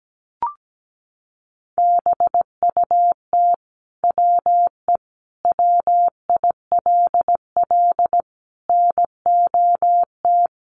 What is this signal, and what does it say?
Morse code, heard as a keyed tone, about 17 wpm, 700 Hz. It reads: BUT WE WILL NOT
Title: Sentences from Top 1200 Words 17wpm